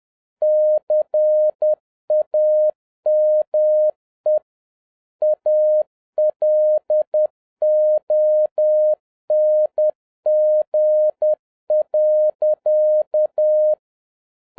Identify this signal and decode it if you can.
Morse code, heard as a keyed tone, about 10 wpm, 610 Hz. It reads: CAME ALONG.
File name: 2011-07-mp3 / n46 JOHN K3TN 10wpm